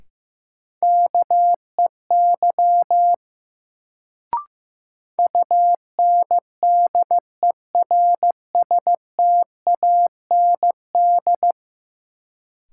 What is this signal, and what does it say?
Morse code, heard as a keyed tone, about 15 wpm, 700 Hz. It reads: KEY UNDERSTAND
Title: Sentences from Top 600 Words - New Word Drill - 15wpm